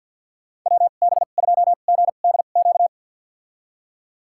Morse code, heard as a keyed tone, about 46 wpm, spaced as for 35 wpm, 700 Hz. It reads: WB2GDQ
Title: USA Call Signs - 2x Character Spacing - 50wpm